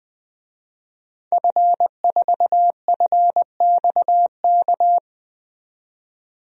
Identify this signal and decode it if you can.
Morse code, heard as a keyed tone, about 20 wpm, 700 Hz. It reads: F4FXK
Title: DXCC-100 Call Signs - 5 Characters - 20wpm